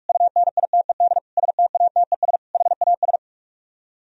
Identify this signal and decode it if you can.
Morse code, heard as a keyed tone, about 50 wpm, 700 Hz. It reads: UNITED STATES HAS